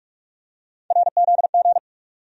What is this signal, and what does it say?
Morse code, heard as a keyed tone, about 46 wpm, 700 Hz. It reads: RZG
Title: USA Call Sign Postfixes - 3 Letters 45wpm